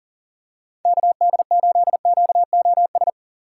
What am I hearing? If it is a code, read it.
Morse code, heard as a keyed tone, about 37 wpm, 700 Hz. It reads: KD8QOS
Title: USA Call Signs - 6 Characters - 4x Character Spacing - 40wpm